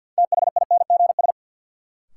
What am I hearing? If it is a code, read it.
Morse code, heard as a keyed tone, about 52 wpm, 700 Hz. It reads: THINGS
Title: Sentences from Top 1000 Words - Review Part 4 - New Word Drill - 50wpm